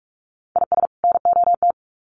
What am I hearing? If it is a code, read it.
Morse code, heard as a keyed tone, about 44 wpm, 700 Hz. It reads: IS NOT
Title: Sets of 2 Words 45wpm